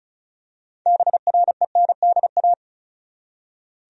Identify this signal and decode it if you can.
Morse code, heard as a keyed tone, about 36 wpm, 700 Hz. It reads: BRENDA
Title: USA Common Names - 35wpm